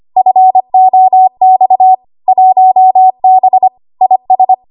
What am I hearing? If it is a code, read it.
Morse code, heard as a keyed tone, about 25 wpm, 750 Hz. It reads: FOX 1B IS
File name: Archive / 25 WPM / 25WPM